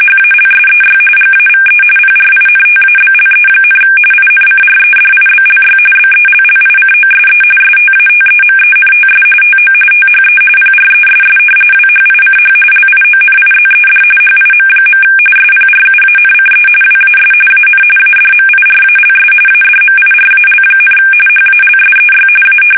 RTTY, about 45.45 baud. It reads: HALSEY GAVE PRINTABLE, INCREDIBLY BELLICOSE STATEMENTS TO THE PRESS WHICH LIONIZED HIM. SPRUANCE WAS AN ADMIRALS ADMIRAL, HALSEY A
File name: fsk850 / battleshipnj